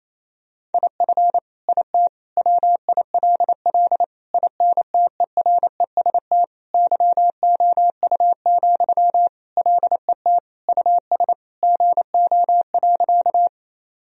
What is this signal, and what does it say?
Morse code, heard as a keyed tone, about 28 wpm, 700 Hz. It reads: IF IT WILL INTEREST YOU, LET US GO.
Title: Sentences from Top 1000 Words - Review Part 2 - 28wpm